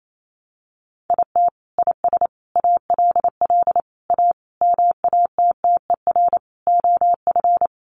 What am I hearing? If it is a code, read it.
Morse code, heard as a keyed tone, about 28 wpm, 700 Hz. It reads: IT IS ALL A MATTER OF
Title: Sentences from Top 1300 Words 28wpm